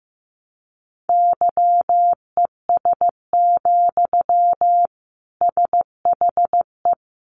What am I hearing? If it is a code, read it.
Morse code, heard as a keyed tone, about 15 wpm, 700 Hz. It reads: YES, SHE
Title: Sentences from Top 700 Words 15wpm